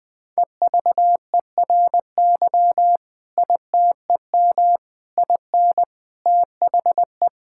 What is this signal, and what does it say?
Morse code, heard as a keyed tone, about 20 wpm, 700 Hz. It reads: EVERY ITEM IN THE
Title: Sentences from Top 1000 Words - Review Part 6 - 20wpm